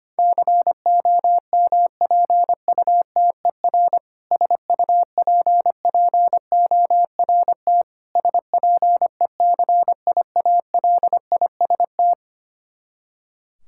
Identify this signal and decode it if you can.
Morse code, heard as a keyed tone, about 25 wpm, 700 Hz. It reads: COMPUTER SUPPORT SPECIALIST
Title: Practice QSO - Exchange 3 - More Info - 25wpm